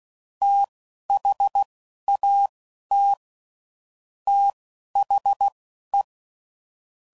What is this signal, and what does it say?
Morse code, heard as a keyed tone, about 16 wpm, spaced as for 11 wpm, 800 Hz. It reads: THAT THE
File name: CW Code Groups @10WPM / Technical Subjects / AND Gate